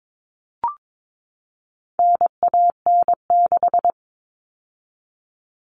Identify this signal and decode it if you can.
Morse code, heard as a keyed tone, about 22 wpm, 700 Hz. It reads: NAN6
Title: All Cut Numbers - 4-Digit 22wpm